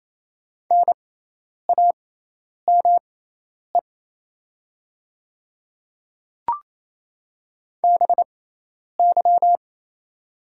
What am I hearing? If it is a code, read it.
Morse code, heard as a keyed tone, about 28 wpm, spaced as for 10 wpm, 700 Hz. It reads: NAME BY
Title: 6x Character Spacing - Top 200 Words - 28wpm